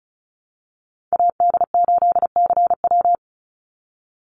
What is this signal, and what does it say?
Morse code, heard as a keyed tone, about 34 wpm, 700 Hz. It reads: AD8CW
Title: Call Signs Repeated 3x - 35wpm